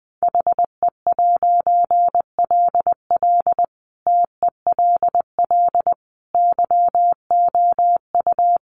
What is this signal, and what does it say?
Morse code, heard as a keyed tone, about 20 wpm, 700 Hz. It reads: HE'LL TELL YOU
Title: A Sound Of Thunder 20wpm